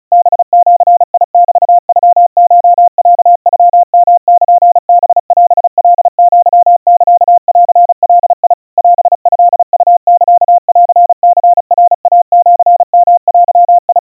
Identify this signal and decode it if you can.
Morse code (Japanese Wabun code), heard as a keyed tone, about 35 wpm, 700 Hz. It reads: ハジメノコロノヨルハオカアサンガ オトウサンニナイシヨデ
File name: kawa / Morse sample / kamyu 130 cpm 700 Hz Crispy